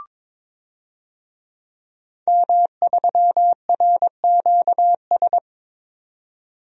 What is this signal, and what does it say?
Morse code, heard as a keyed tone, about 22 wpm, 700 Hz. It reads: M3RQS
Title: DXCC-100 Call Signs - 5 Characters - 22wpm